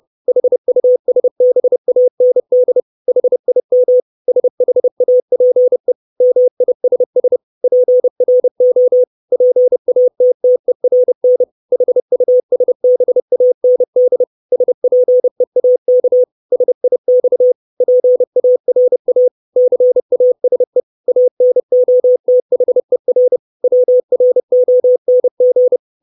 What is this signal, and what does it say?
Morse code, heard as a keyed tone, about 30 wpm, 500 Hz. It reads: HUSBAND HIM SHAPE MISS PRO PATTERN HUSBAND SPEAK SIX PARA CASE ANOTHER WRONG
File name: 632 words / 632 Words 30WPM